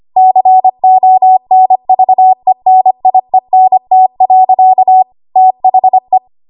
Morse code, heard as a keyed tone, about 25 wpm, 750 Hz. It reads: CONVENIENT. THE
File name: Archive / 25 WPM / 25WPM